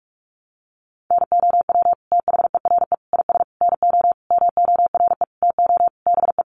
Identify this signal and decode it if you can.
Morse code, heard as a keyed tone, about 44 wpm, 700 Hz. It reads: NOW THERE IS NO MORE TO BE
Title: Sentences from Top 100 Words 45wpm